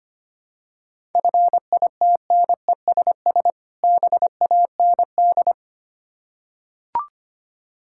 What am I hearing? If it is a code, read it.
Morse code, heard as a keyed tone, about 25 wpm, 700 Hz. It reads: FITNESS BAND
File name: practice / 25wpm / Difficult Words with Hints - 25wpm